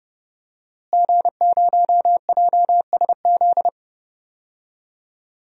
Morse code, heard as a keyed tone, about 30 wpm, 700 Hz. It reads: G0JSZ
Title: DXCC-100 Call Signs - 5 Characters - Repeated 2x - 30wpm